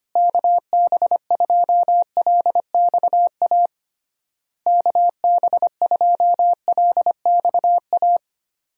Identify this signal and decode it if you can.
Morse code, heard as a keyed tone, about 25 wpm, 700 Hz. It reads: KB2LXA KB2LXA
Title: USA Call Signs - 6 Characters - Repeated 3x - 25wpm